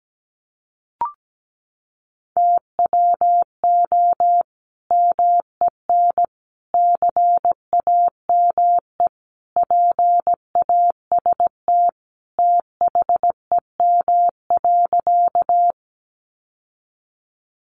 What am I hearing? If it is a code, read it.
Morse code, heard as a keyed tone, about 17 wpm, 700 Hz. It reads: TWO MEN CAME PAST THEM.
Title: Sentences from Top 600 Words 17wpm